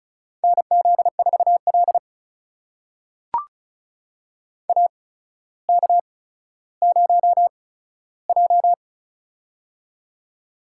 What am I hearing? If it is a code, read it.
Morse code, heard as a keyed tone, about 36 wpm, spaced as for 15 wpm, 700 Hz. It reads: NZ4L AK0J
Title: USA Call Signs - 4 Characters - 8x Character Spacing - 35wpm